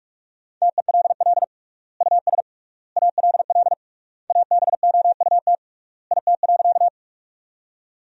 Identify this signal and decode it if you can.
Morse code, heard as a keyed tone, about 46 wpm, 700 Hz. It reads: TELL US ALL ABOUT IT.
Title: 3x Word-Spacing - Sentences from Top 200 Words - 45wpm